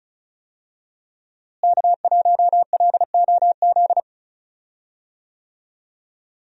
Morse code, heard as a keyed tone, about 34 wpm, 700 Hz. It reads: K1LOZ
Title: USA Call Signs - 5 Characters - Repeated 3x - 35wpm